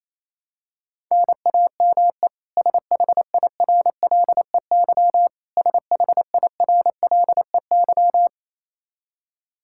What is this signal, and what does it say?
Morse code, heard as a keyed tone, about 28 wpm, 700 Hz. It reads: NAME SHIRLEY SHIRLEY
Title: K1USN SST QSO Practice 28wpm